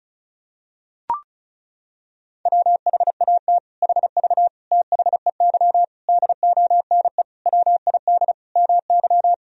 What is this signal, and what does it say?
Morse code, heard as a keyed tone, about 36 wpm, 700 Hz. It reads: WHAT HV THEY DONE WID MY
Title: Sentences from Top 500 Words and Abbreviations 35wpm